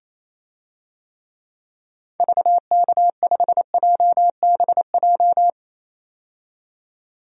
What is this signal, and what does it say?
Morse code, heard as a keyed tone, about 28 wpm, 700 Hz. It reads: VK5JBJ
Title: DXCC-100 Call Signs - 6 Characters - 28wpm